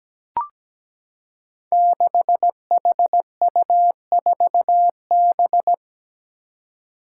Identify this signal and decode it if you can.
Morse code, heard as a keyed tone, about 17 wpm, 700 Hz. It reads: 6HU4B
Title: Five Challenging Letter-Number Combinations 17wpm